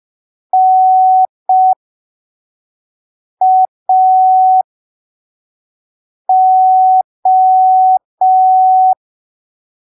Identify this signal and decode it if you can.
Morse code, heard as a keyed tone, about 5 wpm, spaced as for 3 wpm, 750 Hz. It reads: NAO
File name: arquivos / texto / texto 02 5ppm